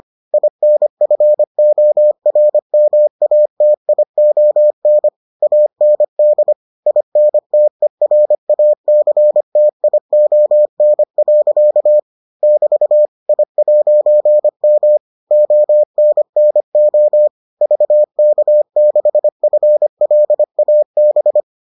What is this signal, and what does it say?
Morse code, heard as a keyed tone, about 25 wpm, 600 Hz. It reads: INFORMATION AND INTERACTION. <BT> I'M ONNO VK6FLAB